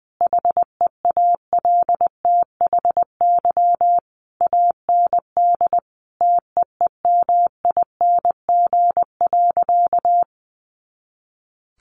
Morse code, heard as a keyed tone, about 20 wpm, 700 Hz. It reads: HEALTHY AND TEEMING.
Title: A Sound Of Thunder 20wpm